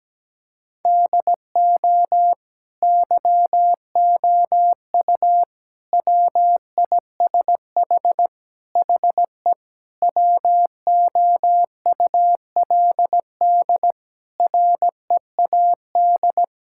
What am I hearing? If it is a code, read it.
Morse code, heard as a keyed tone, about 17 wpm, 700 Hz. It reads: DO YOU WISH HE WOULD READ
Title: Sentences from Top 1600 Words 17wpm